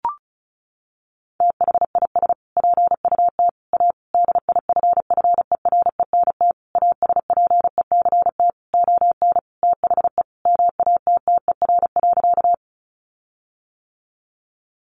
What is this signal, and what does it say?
Morse code, heard as a keyed tone, about 34 wpm, 700 Hz. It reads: THIS PUT A DIFFERENT ASPECT ON THE MATTER.
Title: Sentences from Top 1300 Words 35wpm